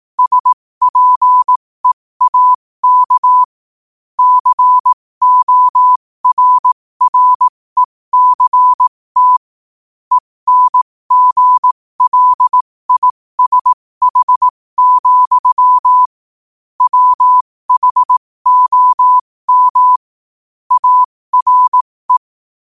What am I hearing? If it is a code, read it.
Morse code, heard as a keyed tone, about 18 wpm, spaced as for 15 wpm, 1000 Hz. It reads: SPEAK CORRECT ENGLISH, WHOM ARE
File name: qotdfiles / 15wpm